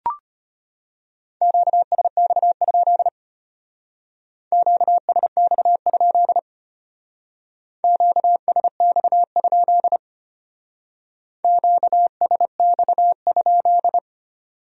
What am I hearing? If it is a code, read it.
Morse code, heard as a keyed tone, about 32 wpm, 700 Hz. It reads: QSX? QSX? QSX? QSX?